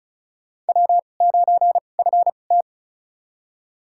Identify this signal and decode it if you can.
Morse code, heard as a keyed tone, about 36 wpm, spaced as for 25 wpm, 700 Hz. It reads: W9FT
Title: USA Call Signs - 4 Characters - 2x Character Spacing - 35wpm